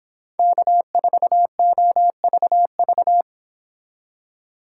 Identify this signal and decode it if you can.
Morse code, heard as a keyed tone, about 28 wpm, 700 Hz. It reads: K4OVV